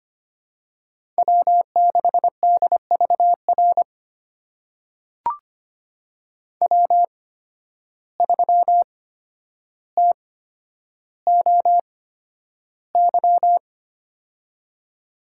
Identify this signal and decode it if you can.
Morse code, heard as a keyed tone, about 25 wpm, spaced as for 11 wpm, 700 Hz. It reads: W6DVR W3TOY
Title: USA Call Signs - 5 Characters - 8x Character Spacing - 25wpm